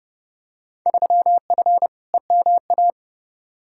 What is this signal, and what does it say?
Morse code, heard as a keyed tone, about 30 wpm, 700 Hz. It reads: 3F EMA
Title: Field Day Exchanges Encore - 30wpm